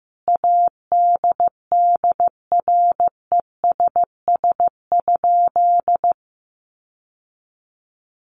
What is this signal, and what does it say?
Morse code, heard as a keyed tone, about 14 wpm, 700 Hz. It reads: ADDRESS?
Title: Sentences from Top 1000 Words 15wpm